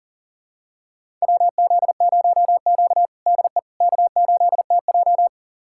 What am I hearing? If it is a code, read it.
Morse code, heard as a keyed tone, about 40 wpm, 700 Hz. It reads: WZ0Q DE K8TJ
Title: USA Callsign DE Callsign - 4 Characters - 40wpm